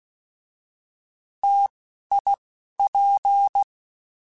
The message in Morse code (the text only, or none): T I P